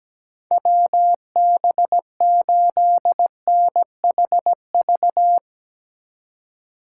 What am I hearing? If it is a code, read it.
Morse code, heard as a keyed tone, about 17 wpm, 700 Hz. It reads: WB8NHV
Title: USA Call Signs - 6 Characters - 3x Character Spacing - 17wpm